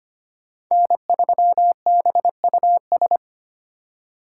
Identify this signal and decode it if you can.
Morse code, heard as a keyed tone, about 33 wpm, 700 Hz. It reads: N3BUS